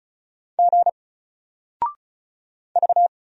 Morse code, heard as a keyed tone, about 36 wpm, 700 Hz. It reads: G V
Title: Single Character (Letter:Number:Punctuation:Prosign) 35wpm